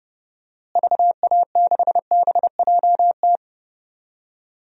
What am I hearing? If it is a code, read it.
Morse code, heard as a keyed tone, about 29 wpm, 700 Hz. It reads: VA6BJT
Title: DXCC-100 Call Signs - 6 Characters - Repeated 2x - 30wpm